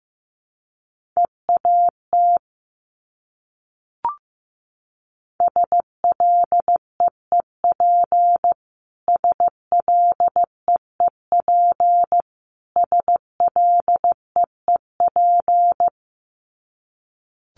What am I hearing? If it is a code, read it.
Morse code, heard as a keyed tone, about 15 wpm, 700 Hz. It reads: EAT SLEEP SLEEP SLEEP
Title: Top 500 Words Repeated 15wpm